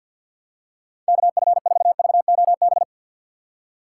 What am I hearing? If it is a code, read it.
Morse code, heard as a keyed tone, about 53 wpm, 700 Hz. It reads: KF4VOB
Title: USA Call Signs - 6 Characters - 4x Character Spacing - 50wpm